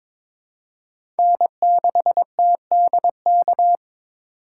Morse code, heard as a keyed tone, about 22 wpm, 700 Hz. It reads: N6TDK
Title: USA Call Signs - 5 Characters - 8x Character Spacing - 22wpm